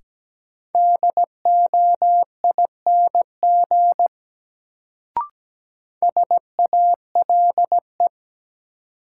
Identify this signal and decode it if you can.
Morse code, heard as a keyed tone, about 17 wpm, 700 Hz. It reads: DOING SALE
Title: Sentences from Top 1400 Words - New Word Drill Alt - 17wpm